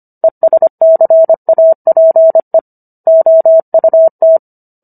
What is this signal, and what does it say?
Morse code, heard as a keyed tone, about 25 wpm, 650 Hz. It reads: ESCAPE OUT